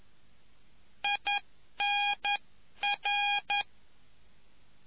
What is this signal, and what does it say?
Morse code, heard as a keyed tone, about 11 wpm, 820 Hz. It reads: INR